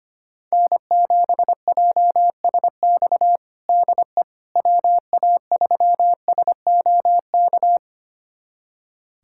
Morse code, heard as a keyed tone, about 25 wpm, 700 Hz. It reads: N7JSX DE WA3SOK
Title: USA Callsign DE Callsign 25wpm